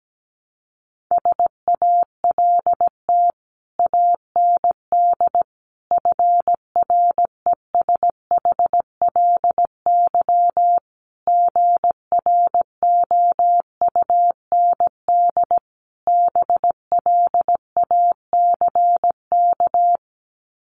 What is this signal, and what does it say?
Morse code, heard as a keyed tone, about 17 wpm, 700 Hz. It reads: SALT AND FRESHLY GROUND BLACK
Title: Sets of 5 Words Encore 17wpm